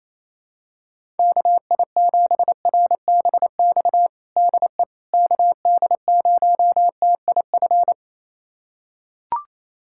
Morse code, heard as a keyed tone, about 28 wpm, 700 Hz. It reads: KI7RBX DE KD0TIF
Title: USA Callsign DE Callsign - 6 Characters - 28wpm